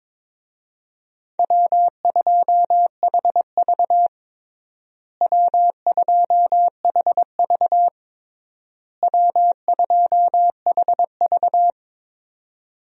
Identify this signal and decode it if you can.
Morse code, heard as a keyed tone, about 22 wpm, 700 Hz. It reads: W2HV W2HV W2HV
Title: USA Call Signs - 4 Characters - Repeated 3x - 22wpm